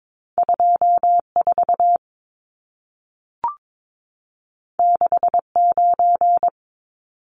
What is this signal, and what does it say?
Morse code, heard as a keyed tone, about 22 wpm, 700 Hz. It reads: 24 69
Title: Numbers - 2-Digit 22wpm